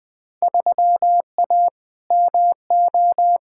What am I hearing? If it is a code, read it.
Morse code, heard as a keyed tone, about 20 wpm, 700 Hz. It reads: 3A MO
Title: Field Day Exchanges Encore - 20wpm